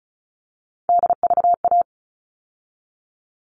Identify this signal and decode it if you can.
Morse code, heard as a keyed tone, about 34 wpm, 700 Hz. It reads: DVA
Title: All Cut Numbers - 3-Digit 35wpm